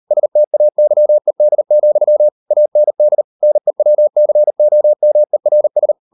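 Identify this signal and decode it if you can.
Morse code, heard as a keyed tone, about 40 wpm, 600 Hz. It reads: STAYED, AND NEWCOMERS
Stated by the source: Morse code practice at 38 WPM, sending Ken Follett's novel Never in English, from CW med Gnister.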